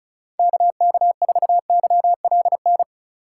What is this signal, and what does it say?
Morse code, heard as a keyed tone, about 34 wpm, 700 Hz. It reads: KK4YLN
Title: USA Call Signs - 6 Characters - 2x Character Spacing - 35wpm